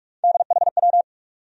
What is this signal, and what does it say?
Morse code, heard as a keyed tone, about 42 wpm, 700 Hz. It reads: DHW
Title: USA Call Sign Postfixes - 3 Letters 45wpm